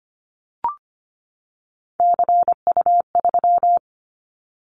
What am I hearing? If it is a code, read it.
Morse code, heard as a keyed tone, about 25 wpm, 700 Hz. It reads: CU3
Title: Callsign Prefixes for Top 100 DXCC - 25wpm